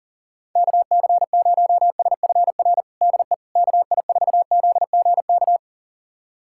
Morse code, heard as a keyed tone, about 40 wpm, 700 Hz. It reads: KC0SFR DE KI4ZGK
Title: USA Callsign DE Callsign - 6 Characters - 40wpm